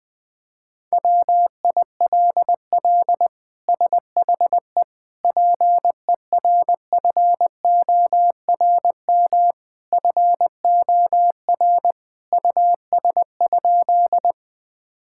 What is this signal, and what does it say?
Morse code, heard as a keyed tone, about 20 wpm, 700 Hz. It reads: WILL SHE PERFORM FOR US?
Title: Sentences from Top 1000 Words - Review Part 5 - 20wpm